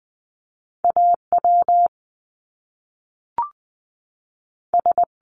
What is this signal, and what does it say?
Morse code, heard as a keyed tone, about 20 wpm, 700 Hz. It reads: AW S
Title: Top Two Letter Combinations in Words 20wpm